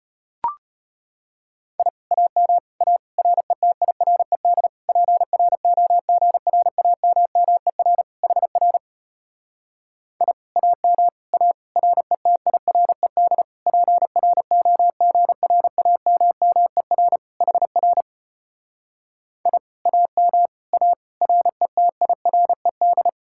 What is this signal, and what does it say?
Morse code, heard as a keyed tone, about 33 wpm, 700 Hz. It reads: I AM A RETIRED PROGRAMMER HR I AM A RETIRED PROGRAMMER HR I AM A RETIRED